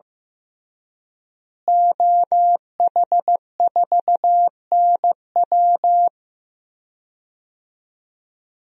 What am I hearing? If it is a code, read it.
Morse code, heard as a keyed tone, about 15 wpm, 700 Hz. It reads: OH4NW
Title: DXCC-100 Call Signs - 5 Characters - Repeated 3x - 15wpm